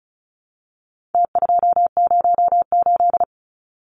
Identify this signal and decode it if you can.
Morse code, heard as a keyed tone, about 34 wpm, 700 Hz. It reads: T208
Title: Numbers - Contest Style - 4-Digit 35wpm